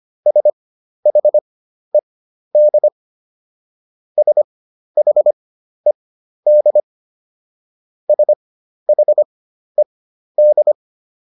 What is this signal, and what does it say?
Morse code, heard as a keyed tone, about 25 wpm, spaced as for 12 wpm, 600 Hz. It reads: SHED SHED SHED